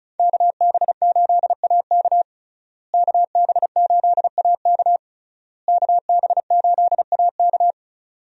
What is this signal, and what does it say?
Morse code, heard as a keyed tone, about 36 wpm, 700 Hz. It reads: KB8AK KB8AK KB8AK
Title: USA Call Signs - 5 Characters - Repeated 3x - 35wpm